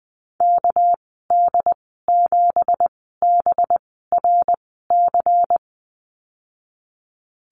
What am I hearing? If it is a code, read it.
Morse code, heard as a keyed tone, about 20 wpm, spaced as for 15 wpm, 700 Hz. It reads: KD7BRC
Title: 200 Lesson 32g - Letter k - Call Signs 20wpm